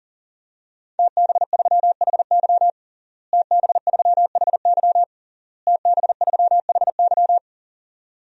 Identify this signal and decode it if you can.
Morse code, heard as a keyed tone, about 40 wpm, 700 Hz. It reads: TB3HY TB3HY TB3HY
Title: DXCC-100 Call Signs - 5 Characters - Repeated 3x - 40wpm